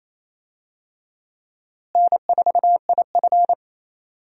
Morse code, heard as a keyed tone, about 28 wpm, 700 Hz. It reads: N4IF
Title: USA Call Signs - 4 Characters - Repeated 2x - 28wpm